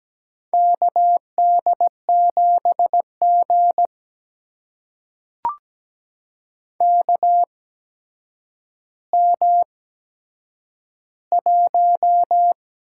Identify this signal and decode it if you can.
Morse code, heard as a keyed tone, about 17 wpm, spaced as for 8 wpm, 700 Hz. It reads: KD7G KM1
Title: USA Call Signs - 4 Characters - 8x Character Spacing - 17wpm